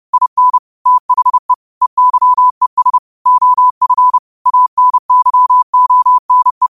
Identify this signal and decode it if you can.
Morse code, heard as a keyed tone, about 30 wpm, 1000 Hz. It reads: IN THE EYES OF ANYONE
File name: qotdfiles / 30wpm